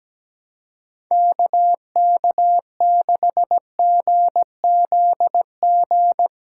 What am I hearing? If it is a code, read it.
Morse code, heard as a keyed tone, about 17 wpm, 700 Hz. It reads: KK6GZG
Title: USA Callsign DE Callsign - 6 Characters - 17wpm